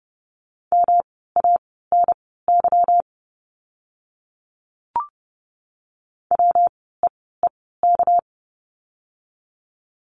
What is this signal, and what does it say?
Morse code, heard as a keyed tone, about 30 wpm, spaced as for 17 wpm, 700 Hz. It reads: MANY WEEK
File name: course / 30:10wpm / 230 Lesson 37f - Letter x - Words 30wpm